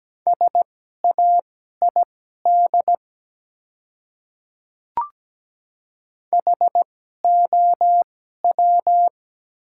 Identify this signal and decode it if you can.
Morse code, heard as a keyed tone, about 17 wpm, spaced as for 12 wpm, 700 Hz. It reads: SAID HOW
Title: 2x Character Spacing - Top 100 Words - 17wpm